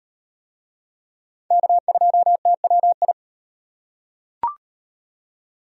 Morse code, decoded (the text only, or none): K2TWI